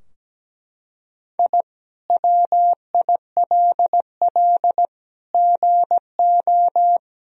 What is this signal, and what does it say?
Morse code, heard as a keyed tone, about 17 wpm, 700 Hz. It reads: I WILL GO
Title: Sentences from Top 1000 Words - Review Part 4 - 17wpm